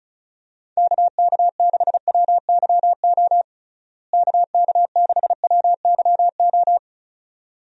Morse code, heard as a keyed tone, about 34 wpm, 700 Hz. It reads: KK6WYO KK6WYO
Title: USA Call Signs - 6 Characters - Repeated 2x - 35wpm